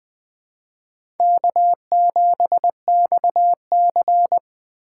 Morse code, heard as a keyed tone, about 20 wpm, 700 Hz. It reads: K7XC